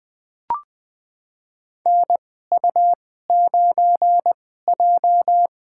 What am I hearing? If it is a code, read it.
Morse code, heard as a keyed tone, about 20 wpm, spaced as for 15 wpm, 700 Hz. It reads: NU9J
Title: USA Call Signs - 4 Characters - 2x Character Spacing - 20wpm